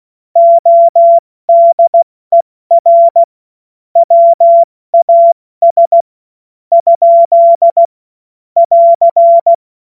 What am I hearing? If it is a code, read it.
Morse code, heard as a keyed tone, about 16 wpm, spaced as for 14 wpm, 680 Hz. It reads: ODER WAS ? <AR>
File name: Morsekurs / Witze / Witz 11. Juli 2023 70 F80